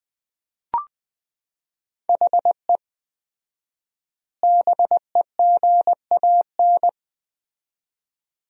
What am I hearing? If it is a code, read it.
Morse code, heard as a keyed tone, about 21 wpm, 700 Hz. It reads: HE BEGAN
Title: 4x Word-Spacing - Sentences from Top 500 Words - 20wpm